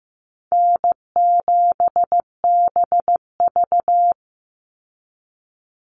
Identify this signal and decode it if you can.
Morse code, heard as a keyed tone, about 15 wpm, 700 Hz. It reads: N7BV
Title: Call Signs Repeated 2x - 15wpm